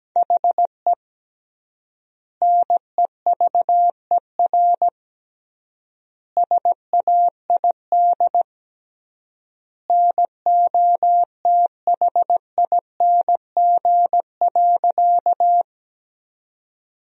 Morse code, heard as a keyed tone, about 17 wpm, 700 Hz. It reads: HE NEVER SAID NOTHING.
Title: 3x Word-Spacing - Sentences from Top 500 Words - 17wpm